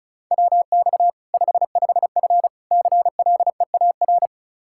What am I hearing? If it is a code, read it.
Morse code, heard as a keyed tone, about 36 wpm, 700 Hz. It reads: WX 55F CLEAR
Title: Practice QSO - Exchange 2 - Info - 35wpm